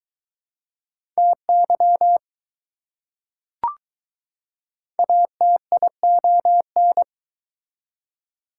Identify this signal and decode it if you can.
Morse code, heard as a keyed tone, about 21 wpm, 700 Hz. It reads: TY ATION